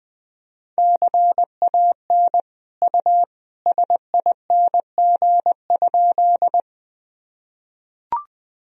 Morse code, heard as a keyed tone, about 21 wpm, 700 Hz. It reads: CAN U SING?
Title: Sentences from Top 400 Words and Abbreviations 20wpm